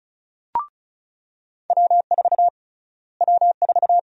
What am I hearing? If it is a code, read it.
Morse code, heard as a keyed tone, about 36 wpm, 700 Hz. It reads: W4 W4
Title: USA Call Sign Prefixes - Repeated 2x - 35wpm